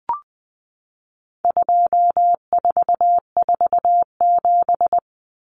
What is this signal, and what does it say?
Morse code, heard as a keyed tone, about 20 wpm, 700 Hz. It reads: 2447
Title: Numbers - 4-Digit 20wpm